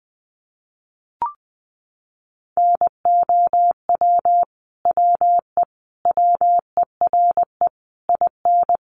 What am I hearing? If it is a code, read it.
Morse code, heard as a keyed tone, about 20 wpm, 700 Hz. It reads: NOW WE WERE IN
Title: Sentences from Top 1200 Words 20wpm